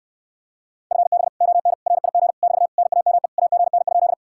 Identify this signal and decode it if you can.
Morse code, heard as a keyed tone, about 69 wpm, 700 Hz. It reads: FB ON RIG <BT> NAME AGN?